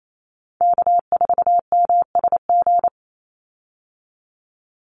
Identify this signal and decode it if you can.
Morse code, heard as a keyed tone, about 28 wpm, 700 Hz. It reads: K4MSG
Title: Call Signs Repeated 2x - 28wpm